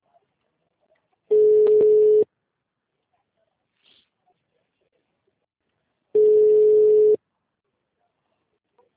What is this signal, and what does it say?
Ringback tone (Europe, 425 Hz).
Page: Rio 3 - Ura